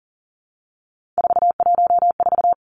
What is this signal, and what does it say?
Morse code, heard as a keyed tone, about 40 wpm, 700 Hz. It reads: 414
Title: Numbers - 3-Digit 40wpm